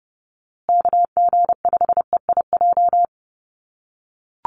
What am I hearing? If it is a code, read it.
Morse code, heard as a keyed tone, about 30 wpm, 700 Hz. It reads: KG5EIJ
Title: USA Call Signs Repeated 3x 30wpm